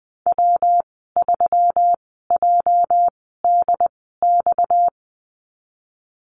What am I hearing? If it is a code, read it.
Morse code, heard as a keyed tone, about 20 wpm, spaced as for 15 wpm, 700 Hz. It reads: W3JDX
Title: 238 Lesson 38g - Letter q - Call Signs 20wpm